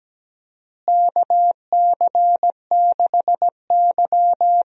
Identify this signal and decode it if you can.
Morse code, heard as a keyed tone, about 17 wpm, 700 Hz. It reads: KC6Y
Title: USA Call Signs - 4 Characters - Repeated 3x - 17wpm